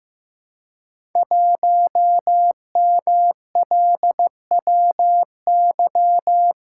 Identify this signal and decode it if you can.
Morse code, heard as a keyed tone, about 15 wpm, 700 Hz. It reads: 1MLWY
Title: Code Groups - 5 Letter:Numbers - 15wpm